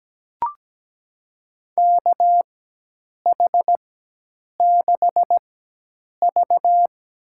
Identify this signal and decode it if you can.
Morse code, heard as a keyed tone, about 17 wpm, spaced as for 8 wpm, 700 Hz. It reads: KH6V
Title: USA Call Signs - 4 Characters - 4x Character Spacing - 17wpm